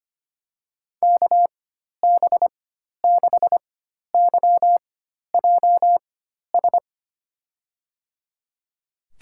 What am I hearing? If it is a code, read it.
Morse code, heard as a keyed tone, about 25 wpm, spaced as for 12 wpm, 700 Hz. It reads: KB6YJS
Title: USA Call Signs - 6 Characters - 4x Character Spacing - 25wpm